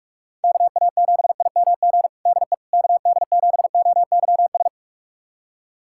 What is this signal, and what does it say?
Morse code, heard as a keyed tone, about 42 wpm, 700 Hz. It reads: KA7IMG DE KD7OYS
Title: USA Callsign DE Callsign - 6 Characters - 45wpm